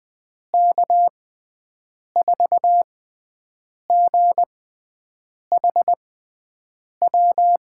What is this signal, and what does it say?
Morse code, heard as a keyed tone, about 21 wpm, spaced as for 7 wpm, 700 Hz. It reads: K4GHW
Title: USA Call Signs - 6 Characters - 6x Character Spacing - 20wpm